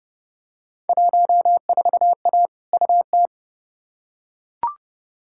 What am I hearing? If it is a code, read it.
Morse code, heard as a keyed tone, about 30 wpm, 700 Hz. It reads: 14A UT
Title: Field Day Exchanges Encore - 30wpm